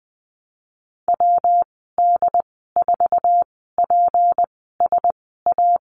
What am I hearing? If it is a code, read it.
Morse code, heard as a keyed tone, about 20 wpm, spaced as for 15 wpm, 700 Hz. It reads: WD4PSA
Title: 154 Lesson 25g - Letter p - Call Signs 20wpm